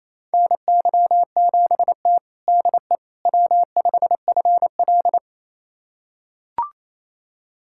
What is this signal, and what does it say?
Morse code, heard as a keyed tone, about 28 wpm, 700 Hz. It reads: NY7T DE W5FL
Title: USA Callsign DE Callsign - 4 Characters - 28wpm